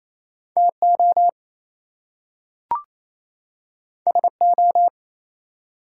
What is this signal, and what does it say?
Morse code, heard as a keyed tone, about 28 wpm, 700 Hz. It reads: TO SO
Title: Two Letter Words - 28wpm